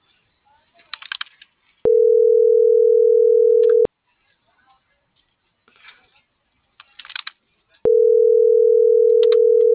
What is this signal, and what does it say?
Ringback tone (North America), 440 + 480 Hz.
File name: CENTRAL AGUILERA / out brenda.ramos